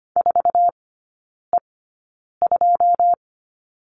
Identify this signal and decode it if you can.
Morse code, heard as a keyed tone, about 25 wpm, spaced as for 9 wpm, 690 Hz. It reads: 4E2
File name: Lesson 14e - Number 2 - Three Characters 25wpm